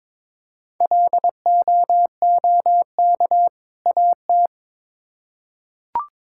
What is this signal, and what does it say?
Morse code, heard as a keyed tone, about 22 wpm, 700 Hz. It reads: LOOK AT
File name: Build A Phrase - 2-Word Phrases - 22wpm